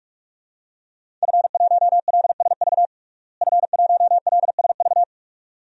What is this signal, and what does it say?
Morse code, heard as a keyed tone, about 46 wpm, 700 Hz. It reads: F1LSV F1LSV
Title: DXCC-100 Call Signs - 5 Characters - Repeated 2x - 45wpm